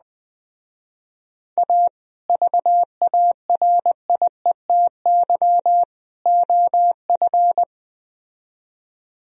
Morse code, heard as a keyed tone, about 19 wpm, 700 Hz. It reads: A VARIETY OF
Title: Speed-Racing - Sets of 3 Words 17wpm